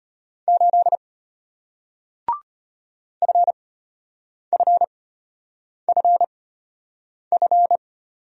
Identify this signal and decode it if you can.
Morse code, heard as a keyed tone, about 35 wpm, spaced as for 8 wpm, 700 Hz. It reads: 8 FFFF